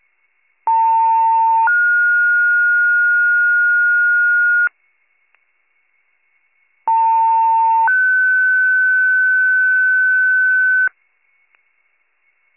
Two-tone page, heard as a one-second tone, then a three-second tone, about 900 then 1430 Hz, about 900 then 1500 Hz. It is heard from 0.7 to 10.9 s.